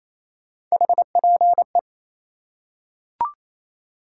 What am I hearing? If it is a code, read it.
Morse code, heard as a keyed tone, about 28 wpm, 700 Hz. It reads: HPE
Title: Top 180 QSO Elements 28wpm